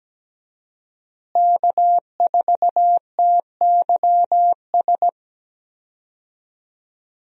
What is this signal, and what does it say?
Morse code, heard as a keyed tone, about 17 wpm, 700 Hz. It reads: K4TYS
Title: USA Call Signs - 5 Characters - Repeated 2x - 17wpm